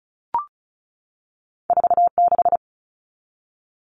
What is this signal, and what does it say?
Morse code, heard as a keyed tone, about 36 wpm, 700 Hz. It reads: 46
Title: All Cut Numbers - 2-Digit 35wpm